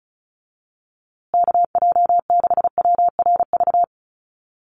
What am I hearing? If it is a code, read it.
Morse code, heard as a keyed tone, about 34 wpm, 700 Hz. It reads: KJ6WRV
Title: USA Call Signs Repeated 2x 35wpm